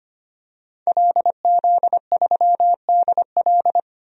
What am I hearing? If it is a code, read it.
Morse code, heard as a keyed tone, about 25 wpm, 690 Hz. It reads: LZ3DL
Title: DXCC-100 Call Signs - 5 Characters - 25wpm